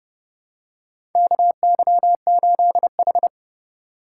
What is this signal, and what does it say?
Morse code, heard as a keyed tone, about 30 wpm, 700 Hz. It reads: KY8H
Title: USA Call Signs - 4x Character Spacing - 30wpm